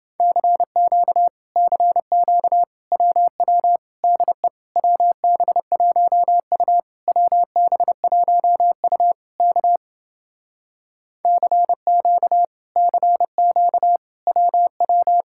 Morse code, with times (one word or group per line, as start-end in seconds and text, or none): 0.2-1.3 CQ
1.6-2.6 CQ
2.9-3.8 WW
4.0-4.5 DE
4.8-6.8 WB1U
7.1-9.1 WB1U
9.4-9.8 K
11.2-12.4 CQ
12.8-14.0 CQ
14.3-15.2 WW